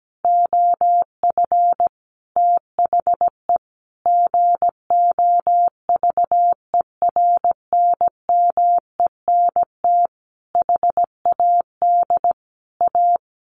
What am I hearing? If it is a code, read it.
Morse code, heard as a keyed tone, about 17 wpm, 700 Hz. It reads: OF THE GOVERNMENT HAD A
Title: Sentences from Top 1400 Words 17wpm